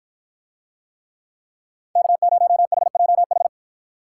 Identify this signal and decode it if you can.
Morse code, heard as a keyed tone, about 53 wpm, 700 Hz. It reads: K0HJH